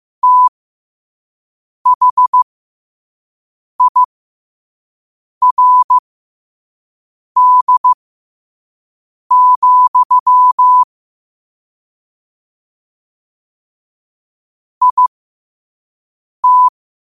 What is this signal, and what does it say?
Morse code, heard as a keyed tone, about 15 wpm, spaced as for 5 wpm, 1000 Hz. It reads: THIRD, IT
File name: qotdfiles / 5wpm